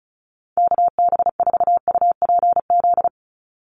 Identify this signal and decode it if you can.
Morse code, heard as a keyed tone, about 34 wpm, 700 Hz. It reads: KB4UPZ
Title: USA Call Signs 35wpm